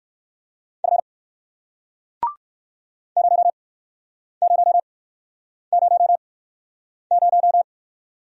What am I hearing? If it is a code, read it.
Morse code, heard as a keyed tone, about 55 wpm, spaced as for 9 wpm, 700 Hz. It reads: F 0000
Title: Speed-Racing Single Letter-Number 45wpm